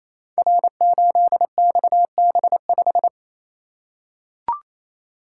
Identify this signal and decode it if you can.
Morse code, heard as a keyed tone, about 28 wpm, 700 Hz. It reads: R8XB5
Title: Code Groups - 5 Letter:Numbers - 28wpm